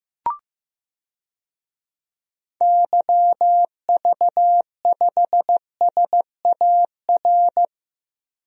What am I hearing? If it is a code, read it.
Morse code, heard as a keyed tone, about 15 wpm, 700 Hz. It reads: YV5SAR
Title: DXCC-100 Call Signs - 6 Characters - Repeated 3x - 15wpm